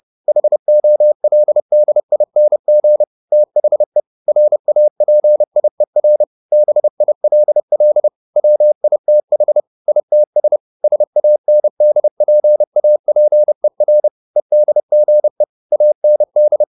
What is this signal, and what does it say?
Morse code, heard as a keyed tone, about 30 wpm, 600 Hz. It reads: HOLDING THE RAPIER BILL WITH ITS SANDPAPER EDGE AND
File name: the old man and the sea / the old man and the sea 1 30wpm